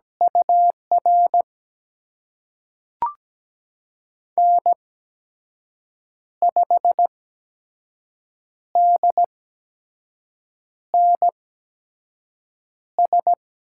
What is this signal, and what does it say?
Morse code, heard as a keyed tone, about 17 wpm, spaced as for 5 wpm, 700 Hz. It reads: UR N5DNS
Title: USA Call Signs - 8x Character Spacing - 17wpm